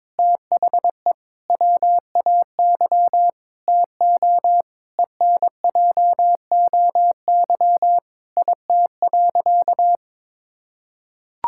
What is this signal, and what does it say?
Morse code, heard as a keyed tone, about 22 wpm, 700 Hz. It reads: THE WAY TO ENJOY IT.
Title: Sentences from Top 1000 Words - Review Part 5 - 22wpm